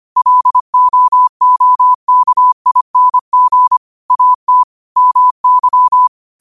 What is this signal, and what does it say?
Morse code, heard as a keyed tone, about 25 wpm, 1000 Hz. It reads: LOOKING AT MY